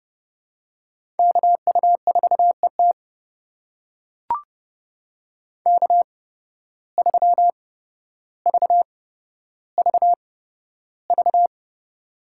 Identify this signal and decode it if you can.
Morse code, heard as a keyed tone, about 30 wpm, spaced as for 13 wpm, 700 Hz. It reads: KU4ET K3VVV
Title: USA Call Signs - 5 Characters - 8x Character Spacing - 30wpm